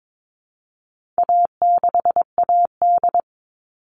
Morse code, heard as a keyed tone, about 22 wpm, 700 Hz. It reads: A6AD
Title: All Cut Numbers - 4-Digit 22wpm